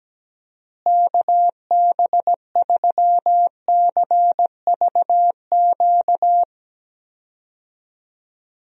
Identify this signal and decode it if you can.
Morse code, heard as a keyed tone, about 17 wpm, 700 Hz. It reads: KB3CVQ
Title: USA Call Signs - 6 Characters - Repeated 3x - 17wpm